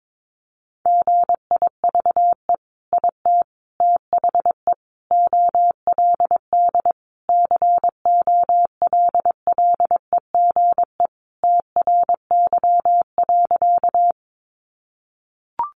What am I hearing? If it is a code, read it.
Morse code, heard as a keyed tone, about 22 wpm, 700 Hz. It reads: GIVE IT THE OLD COLLEGE TRY.
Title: Sentences from Top 1300 Words 22wpm